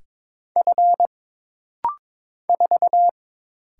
Morse code, heard as a keyed tone, about 22 wpm, 700 Hz. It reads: F 4
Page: This rapid-fire timing is geared to push your ICR (Instant Character Recognition) skills.